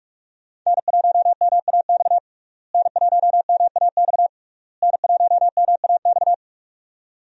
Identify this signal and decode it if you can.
Morse code, heard as a keyed tone, about 41 wpm, 700 Hz. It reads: N1MAX N1MAX N1MAX
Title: USA Call Signs - 5 Characters - Repeated 3x - 45wpm